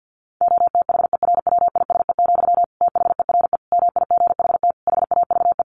Morse code, heard as a keyed tone, about 50 wpm, 700 Hz. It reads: OTHERWISE, THERE MIGHT HAVE
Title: Sentences from Top 1400 Words 50wpm